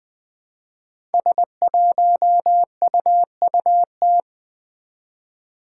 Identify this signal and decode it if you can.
Morse code, heard as a keyed tone, about 21 wpm, 700 Hz. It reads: S1UUT
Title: Five Challenging Letter-Number Combinations 20wpm